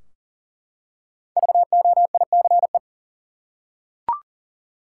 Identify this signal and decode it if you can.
Morse code, heard as a keyed tone, about 40 wpm, 700 Hz. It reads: VOICE
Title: Five Letter Words - 40wpm